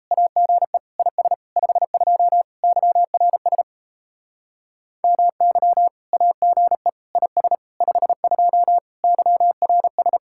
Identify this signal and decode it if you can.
Morse code, heard as a keyed tone, about 37 wpm, 700 Hz. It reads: AGE IS 52 YRS MY AGE IS 52 YRS